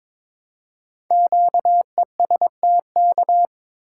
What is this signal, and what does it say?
Morse code, heard as a keyed tone, about 22 wpm, 700 Hz. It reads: QESTK
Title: Code Groups - 5 Letters - 22wpm